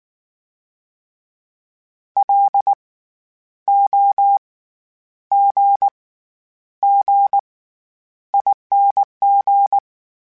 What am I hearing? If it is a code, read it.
Morse code, heard as a keyed tone, about 19 wpm, spaced as for 9 wpm, 800 Hz. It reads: LOGGING
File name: Special INGs / Special INGs-mp3s / SlowToFaster ing @18